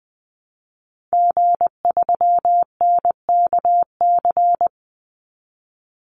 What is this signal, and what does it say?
Morse code, heard as a keyed tone, about 20 wpm, 700 Hz. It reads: G3NKC
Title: Call Signs Repeated 2x - 20wpm